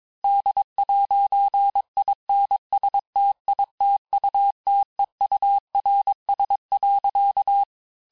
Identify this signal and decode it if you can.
Morse code, heard as a keyed tone, about 22 wpm, 800 Hz. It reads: D'INSTITUTEURS.
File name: texte clair vitesse 16 puis 21 mpm